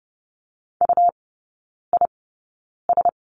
Morse code, heard as a keyed tone, about 30 wpm, spaced as for 9 wpm, 690 Hz. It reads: UIS
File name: Lesson 17e - Letter u - Three Characters 30wpm